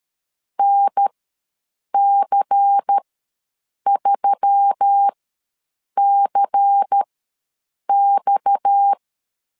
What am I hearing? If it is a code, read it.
Morse code, heard as a keyed tone, about 13 wpm, spaced as for 7 wpm, 800 Hz. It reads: NC3CX
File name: LEZIONI / 6 PKX0 / 6 PKX ws 12wpm 3106